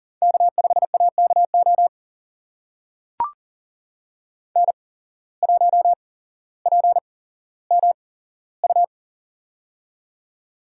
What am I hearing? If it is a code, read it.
Morse code, heard as a keyed tone, about 40 wpm, spaced as for 17 wpm, 700 Hz. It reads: K5AKO N1PMU
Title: USA Call Signs - 5 Characters - 8x Character Spacing - 40wpm